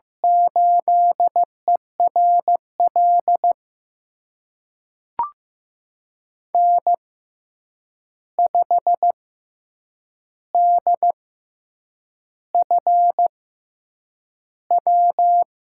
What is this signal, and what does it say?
Morse code, heard as a keyed tone, about 15 wpm, spaced as for 7 wpm, 700 Hz. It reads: 8ERL N5DFW
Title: USA Call Signs - 5 Characters - 6x Character Spacing - 15wpm